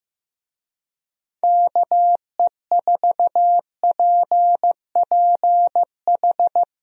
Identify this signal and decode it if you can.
Morse code, heard as a keyed tone, about 15 wpm, 700 Hz. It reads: KE4PPH
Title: USA Call Signs - 6 Characters - Repeated 3x - 15wpm